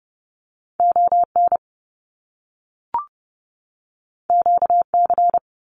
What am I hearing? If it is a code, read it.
Morse code, heard as a keyed tone, about 30 wpm, 700 Hz. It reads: ON QC
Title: Canadian Provinces and Territories 30wpm